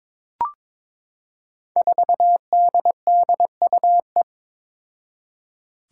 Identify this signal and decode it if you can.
Morse code, heard as a keyed tone, about 22 wpm, 700 Hz. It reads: 4DDUE
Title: Five Challenging Letter-Number Combinations 22wpm